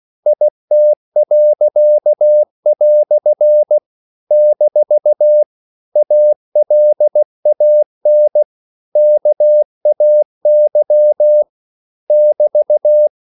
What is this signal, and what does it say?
Morse code, heard as a keyed tone, about 16 wpm, 600 Hz. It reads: IT." - ALAN KAY <BT>
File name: CW / Practice / EnglishQuotations0416